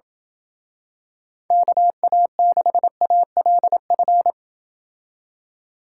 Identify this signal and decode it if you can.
Morse code, heard as a keyed tone, about 27 wpm, 700 Hz. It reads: KA6ALF